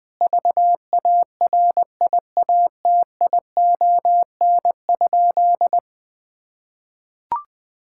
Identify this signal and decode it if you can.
Morse code, heard as a keyed tone, about 20 wpm, 700 Hz. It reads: VARIATION?
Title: Sentences from Top 1600 Words 20wpm